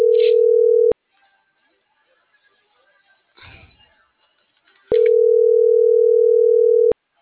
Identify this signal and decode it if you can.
Ringback tone (North America), 440 + 480 Hz.